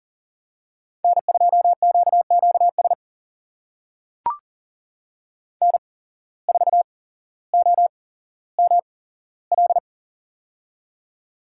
Morse code, heard as a keyed tone, about 40 wpm, spaced as for 17 wpm, 700 Hz. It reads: N2QQS N4OML
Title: USA Call Signs - 5 Characters - 8x Character Spacing - 40wpm